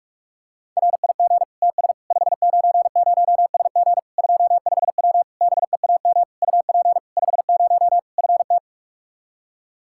Morse code, heard as a keyed tone, about 46 wpm, 700 Hz. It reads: RIG TS 590SG 25W BEAM UP 50 FT
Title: Practice QSO - Exchange 2 - Info - 45wpm